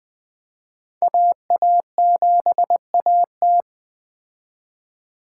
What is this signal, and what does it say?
Morse code, heard as a keyed tone, about 20 wpm, 700 Hz. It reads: AA7AT